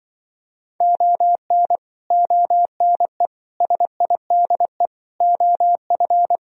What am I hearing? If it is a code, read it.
Morse code, heard as a keyed tone, about 25 wpm, 700 Hz. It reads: ON ONE SIDE OF
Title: Speed-Racing - Sets of 5 Words 20wpm